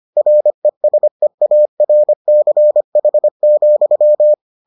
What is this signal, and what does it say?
Morse code, heard as a keyed tone, about 25 wpm, 600 Hz. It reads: RESEARCH,